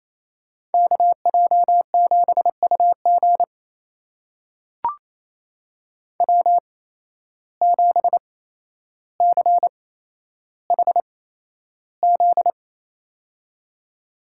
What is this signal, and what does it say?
Morse code, heard as a keyed tone, about 28 wpm, spaced as for 12 wpm, 700 Hz. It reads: KJ7UG W7CHZ
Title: USA Call Signs - 5 Characters - 8x Character Spacing - 28wpm